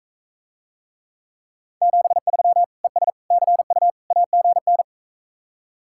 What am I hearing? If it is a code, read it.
Morse code, heard as a keyed tone, about 45 wpm, 700 Hz. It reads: 73 ES CU AGN